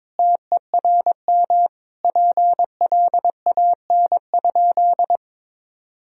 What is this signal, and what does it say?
Morse code, heard as a keyed tone, about 21 wpm, 700 Hz. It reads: TERM PLAN?
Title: Sentences from Top 1000 Words - Review Part 4 - 22wpm